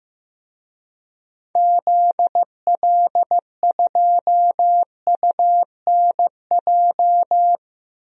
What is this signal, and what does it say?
Morse code, heard as a keyed tone, about 15 wpm, 700 Hz. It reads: ZL2UNJ
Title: DXCC-100 Call Signs - 6 Characters - Repeated 3x - 15wpm